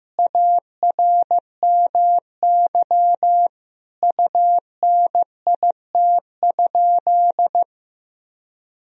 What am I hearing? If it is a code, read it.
Morse code, heard as a keyed tone, about 15 wpm, 700 Hz. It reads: ARMY UNIT?
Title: Sentences from Top 1000 Words - Review Part 2 - 15wpm